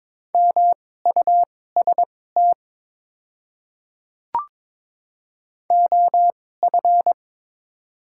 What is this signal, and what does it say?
Morse code, heard as a keyed tone, about 22 wpm, spaced as for 16 wpm, 700 Hz. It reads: MUST OF
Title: 2x Character Spacing - Top 500 Words - 22wpm